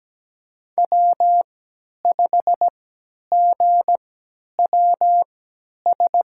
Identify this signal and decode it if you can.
Morse code, heard as a keyed tone, about 17 wpm, spaced as for 10 wpm, 700 Hz. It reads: W5GWS
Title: USA Call Signs - 5 Characters - 3x Character Spacing - 17wpm